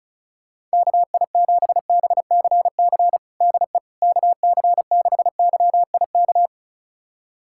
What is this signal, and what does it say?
Morse code, heard as a keyed tone, about 36 wpm, 700 Hz. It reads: KI7BCC DE KC6YIK
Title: USA Callsign DE Callsign 35wpm